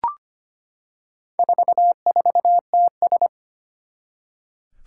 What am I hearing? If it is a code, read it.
Morse code, heard as a keyed tone, about 25 wpm, 700 Hz. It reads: 44TS
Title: Four Challenging Letter-Number Combinations 25wpm